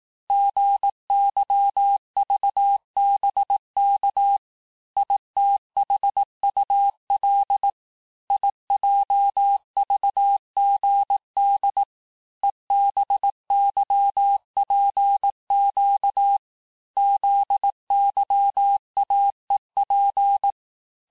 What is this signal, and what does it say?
Morse code, heard as a keyed tone, about 18 wpm, 800 Hz. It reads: GYVBK ITHUL IJVGD EBYPQ ZYAEP
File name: cours-du-lundi files / cours-1594--teste-chiffre-vitesse-17mpm